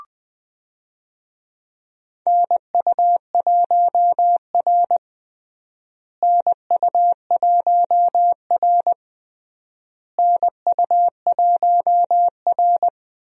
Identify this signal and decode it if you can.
Morse code, heard as a keyed tone, about 20 wpm, 700 Hz. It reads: NU1R NU1R NU1R
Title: USA Call Signs - 4 Characters - Repeated 3x - 20wpm